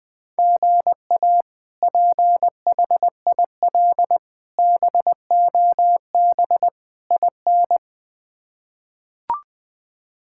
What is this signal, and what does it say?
Morse code, heard as a keyed tone, about 21 wpm, 700 Hz. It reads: GA PHIL BOB IN
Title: K1USN SST QSO Practice - 20wpm